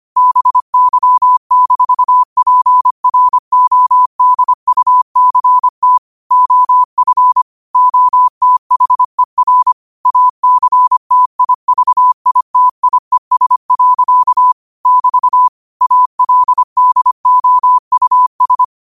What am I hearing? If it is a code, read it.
Morse code, heard as a keyed tone, about 25 wpm, 1000 Hz. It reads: BY-PRODUCT OF OTHER ACTIVITIES. <BT> ALDOUS
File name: qotdfiles / 25wpm